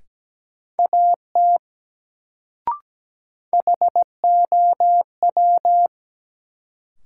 Morse code, heard as a keyed tone, about 17 wpm, 700 Hz. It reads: AT HOW
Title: Sets of 2 Words - New Word Drill - 17wpm